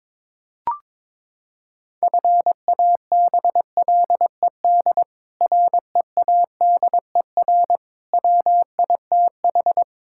Morse code, heard as a keyed tone, about 22 wpm, 700 Hz. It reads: FABLED READER WITH
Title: Pangram Challenge v2 - 22wpm